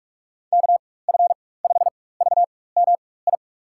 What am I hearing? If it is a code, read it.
Morse code, heard as a keyed tone, about 43 wpm, spaced as for 21 wpm, 700 Hz. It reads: KF5VMI
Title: USA Call Signs - 6 Characters - 4x Character Spacing - 45wpm